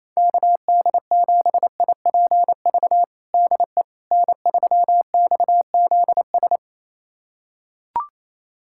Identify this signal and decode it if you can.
Morse code, heard as a keyed tone, about 28 wpm, 700 Hz. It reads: KD7IPV DE N3XZS
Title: USA Callsign DE Callsign 28wpm